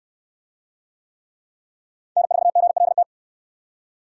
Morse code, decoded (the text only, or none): N4GLT